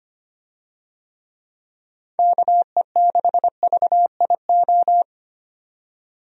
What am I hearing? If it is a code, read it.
Morse code, heard as a keyed tone, about 25 wpm, 700 Hz. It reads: KE6VIO
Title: USA Call Signs - 6 Characters - 25wpm